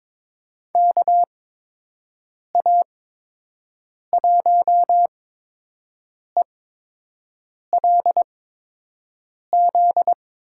Morse code, heard as a keyed tone, about 21 wpm, spaced as for 6 wpm, 700 Hz. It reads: KA1ELZ
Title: USA Call Signs - 6 Characters - 8x Character Spacing - 22wpm